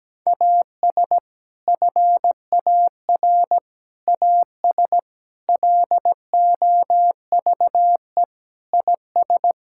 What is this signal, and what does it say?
Morse code, heard as a keyed tone, about 17 wpm, 700 Hz. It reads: AS FAR AS LOVE IS
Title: Sentences from Top 1000 Words - Review Part 6 - 17wpm